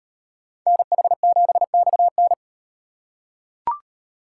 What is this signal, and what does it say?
Morse code, heard as a keyed tone, about 38 wpm, 700 Hz. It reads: NH7XN